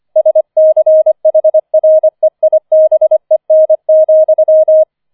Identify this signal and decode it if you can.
Morse code, heard as a keyed tone, about 25 wpm, 630 Hz. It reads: SCHREIBEN,